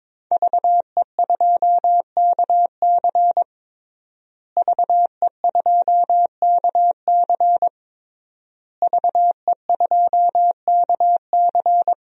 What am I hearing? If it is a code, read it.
Morse code, heard as a keyed tone, about 22 wpm, 700 Hz. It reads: VE2KC VE2KC VE2KC
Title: DXCC-100 Call Signs - 5 Characters - Repeated 3x - 22wpm